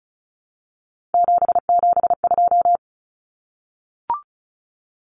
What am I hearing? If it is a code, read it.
Morse code, heard as a keyed tone, about 34 wpm, 700 Hz. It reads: 772
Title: Numbers - 3-Digit 35wpm